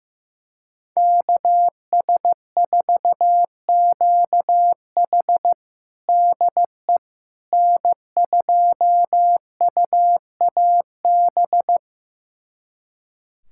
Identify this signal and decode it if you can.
Morse code, heard as a keyed tone, about 15 wpm, 700 Hz. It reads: KS4QH DE N2UAB
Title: USA Callsign DE Callsign - 5 Characters - 15wpm